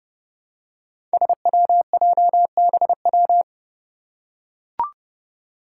Morse code, heard as a keyed tone, about 30 wpm, 700 Hz. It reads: SWJBW
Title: Five Challenging Letter-Number Combinations 30wpm